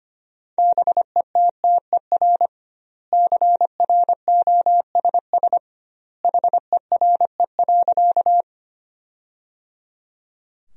Morse code, heard as a keyed tone, about 25 wpm, 700 Hz. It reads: BETTER CROSS HERE.
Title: 2x Word-Spacing - Sentences from Top 400 Words - 25wpm